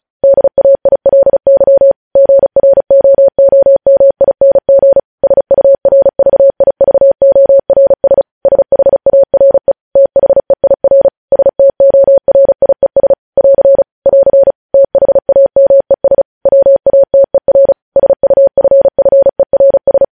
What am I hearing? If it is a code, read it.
Morse code, heard as a keyed tone, about 36 wpm, 550 Hz. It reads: DAILY GROOMING SURVIVORS SHARE THEIR STORIES <AR> <AR> THAMES WATER SUFFERS
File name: podcast / News headlines-1x-35x35